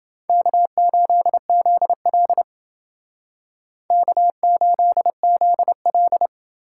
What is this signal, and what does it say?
Morse code, heard as a keyed tone, about 30 wpm, 700 Hz. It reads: K8ZL K8ZL